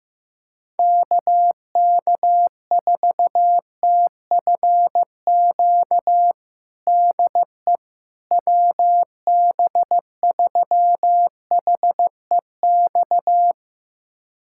Morse code, heard as a keyed tone, about 15 wpm, 700 Hz. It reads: KK4TFQ DE WB3HEX
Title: USA Callsign DE Callsign - 6 Characters - 15wpm